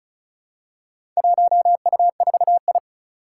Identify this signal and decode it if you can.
Morse code, heard as a keyed tone, about 36 wpm, 700 Hz. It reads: 1U4I
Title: Four Challenging Letter-Number Combinations 35wpm